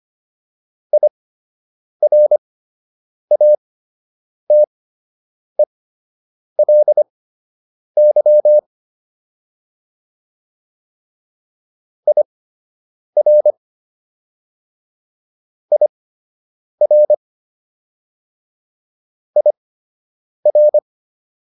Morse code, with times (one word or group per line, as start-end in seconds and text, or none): 0.9-8.6 IRATELY
12.1-13.5 IR
15.7-17.1 IR
19.4-20.8 IR